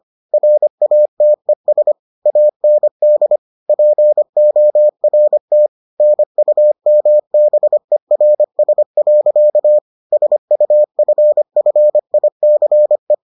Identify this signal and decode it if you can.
Morse code, heard as a keyed tone, about 25 wpm, 600 Hz. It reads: RATES AND PORT NUMBERS. SUFFICE